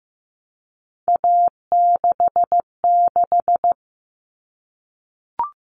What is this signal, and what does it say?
Morse code, heard as a keyed tone, about 15 wpm, 700 Hz. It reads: A66
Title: All Cut Numbers - 3-Digit 15wpm